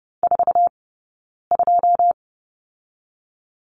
Morse code, heard as a keyed tone, about 30 wpm, 700 Hz. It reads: 4 2
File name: Lesson 18d - Letter m - Two Characters 30wpm